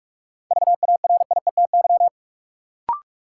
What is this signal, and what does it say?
Morse code, heard as a keyed tone, about 46 wpm, 700 Hz. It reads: VARIETY
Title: Sentences from Top 1000 Words - Review Part 4 - New Word Drill - 45wpm